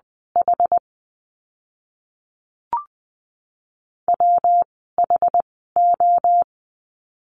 Morse code, heard as a keyed tone, about 20 wpm, spaced as for 15 wpm, 700 Hz. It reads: H WHO
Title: 192 Lesson 31f - Letter v - Words 20wpm